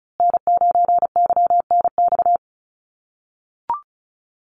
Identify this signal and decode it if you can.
Morse code, heard as a keyed tone, about 34 wpm, 700 Hz. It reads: N9YNX
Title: USA Call Signs Repeated 2x 35wpm